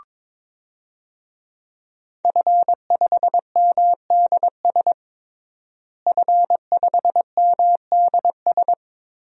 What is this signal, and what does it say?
Morse code, heard as a keyed tone, about 22 wpm, 700 Hz. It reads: F5MDS F5MDS
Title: DXCC-100 Call Signs - 5 Characters - Repeated 3x - 22wpm